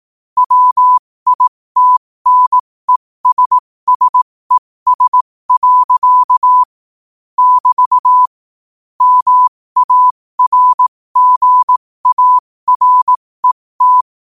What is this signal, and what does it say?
Morse code, heard as a keyed tone, about 18 wpm, spaced as for 15 wpm, 1000 Hz. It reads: WITNESSES. <BT> MARGARET
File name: qotdfiles / 15wpm